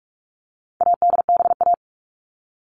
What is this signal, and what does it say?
Morse code, heard as a keyed tone, about 44 wpm, 700 Hz. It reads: ADBA
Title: All Cut Numbers - 4-Digit 45wpm